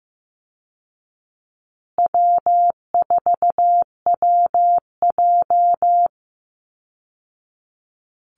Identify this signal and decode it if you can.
Morse code, heard as a keyed tone, about 15 wpm, 700 Hz. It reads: W4WJ
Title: Call Signs Repeated 2x - 15wpm